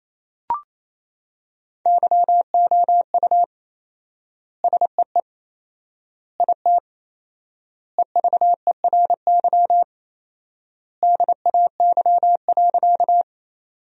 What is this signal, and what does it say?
Morse code, heard as a keyed tone, about 28 wpm, 700 Hz. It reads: YOU SEE IT EVERY DAY.
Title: 4x Word-Spacing - Sentences from Top 200 Words - 28wpm